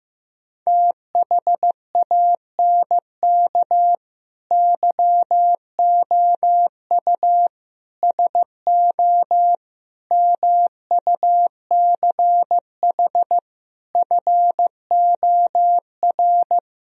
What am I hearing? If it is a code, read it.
Morse code, heard as a keyed tone, about 15 wpm, 700 Hz. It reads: THANK YOU SO MUCH FOR
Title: Speed-Racing - Sets of 5 Words 15wpm